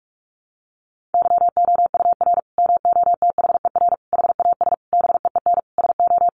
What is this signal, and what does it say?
Morse code, heard as a keyed tone, about 44 wpm, 700 Hz. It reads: YOUR MOTHER HAS BEEN SO
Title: Sentences from Top 1200 Words 45wpm